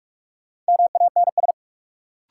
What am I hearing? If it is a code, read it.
Morse code, heard as a keyed tone, about 41 wpm, 700 Hz. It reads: MANS
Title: Sentences from Top 1000 Words - Review Part 6 - New Word Drill - 45wpm